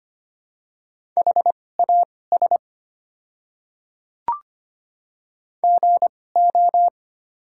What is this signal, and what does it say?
Morse code, heard as a keyed tone, about 25 wpm, spaced as for 18 wpm, 700 Hz. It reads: HAS GO
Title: 2x Character Spacing - Top 300 Words - 25wpm